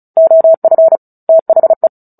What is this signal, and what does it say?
Morse code, heard as a keyed tone, about 36 wpm, 650 Hz. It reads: OF THE
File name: Book Two / 09-Chapter 9 - 35 WPM